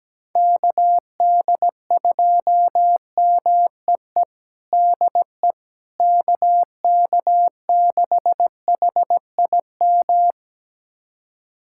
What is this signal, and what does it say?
Morse code, heard as a keyed tone, about 17 wpm, 700 Hz. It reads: KD2MEE DE KK6HIM
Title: USA Callsign DE Callsign - 6 Characters - 17wpm